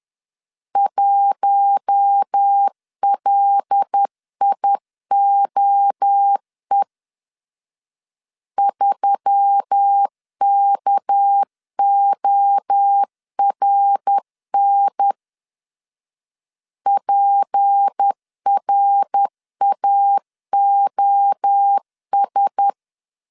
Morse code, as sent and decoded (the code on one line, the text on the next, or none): .---- .-.. .. --- . / ...-- -.- --- .-. -. / .--. .-. .- --- ...
1LIOE 3KORN PRAOS